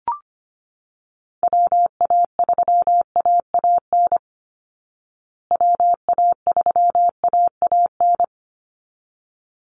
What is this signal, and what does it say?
Morse code, heard as a keyed tone, about 25 wpm, 700 Hz. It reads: WA3AAN WA3AAN
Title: Call Signs Repeated 3x - 25wpm